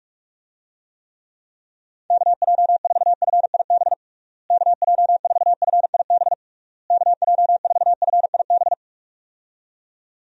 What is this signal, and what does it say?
Morse code, heard as a keyed tone, about 46 wpm, 700 Hz. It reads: KJ4FIB KJ4FIB KJ4FIB
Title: USA Call Signs - 6 Characters - Repeated 3x - 45wpm